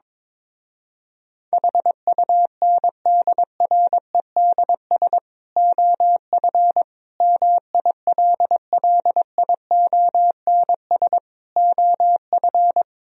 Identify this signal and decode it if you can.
Morse code, heard as a keyed tone, about 22 wpm, 700 Hz. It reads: HUNDREDS OF MILLIONS OF
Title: Speed-Racing - Sets of 5 Words 22wpm